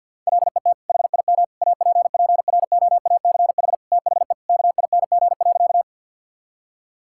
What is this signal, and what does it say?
Morse code, heard as a keyed tone, about 50 wpm, 700 Hz. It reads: LET HIM APPROACH THE KING.
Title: Sentences from Top 1000 Words - Review Part 6 - 50wpm